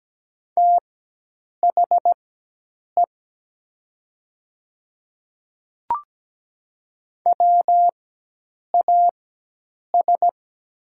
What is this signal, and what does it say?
Morse code, heard as a keyed tone, about 17 wpm, spaced as for 8 wpm, 700 Hz. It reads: THE WAS
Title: 4x Character Spacing - Top 100 Words - 17wpm